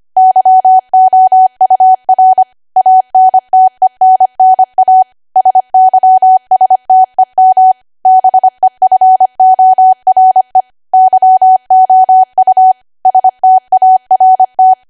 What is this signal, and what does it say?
Morse code, heard as a keyed tone, about 25 wpm, 750 Hz. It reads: YOUR ANTENNA SYSTEM BEFORE YOU START
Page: 25 WPM Code Archive